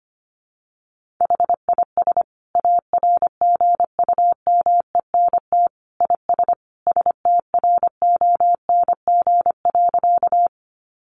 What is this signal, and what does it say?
Morse code, heard as a keyed tone, about 25 wpm, 700 Hz. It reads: HIS ARGUMENT IS STRONG.
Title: Sentences from Top 1400 Words 25wpm